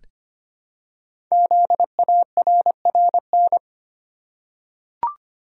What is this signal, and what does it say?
Morse code, heard as a keyed tone, about 25 wpm, 700 Hz. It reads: ZARRN
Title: Code Groups - 5 Letters - 25wpm